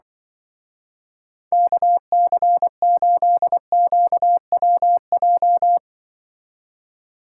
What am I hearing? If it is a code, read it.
Morse code, heard as a keyed tone, about 24 wpm, 700 Hz. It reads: KC8QWJ
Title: Speed Racing - USA 2x3 Call Signs 20wpm